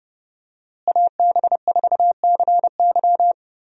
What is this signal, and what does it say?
Morse code, heard as a keyed tone, about 31 wpm, 700 Hz. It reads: AB4CY
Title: USA Call Signs - 5 Characters - Repeated 2x - 30wpm